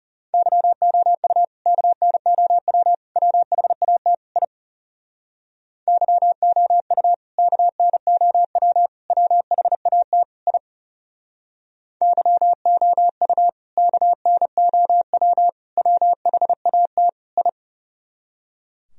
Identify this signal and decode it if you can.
Morse code, heard as a keyed tone, about 40 wpm, 700 Hz. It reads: YOU KNOW WHAT I YOU KNOW WHAT I YOU KNOW WHAT I